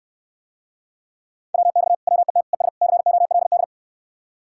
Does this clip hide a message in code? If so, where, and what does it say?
Yes: Morse code from 1.5 to 3.6 s, heard as a keyed tone, about 69 wpm, 700 Hz. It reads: WX WET ES COLD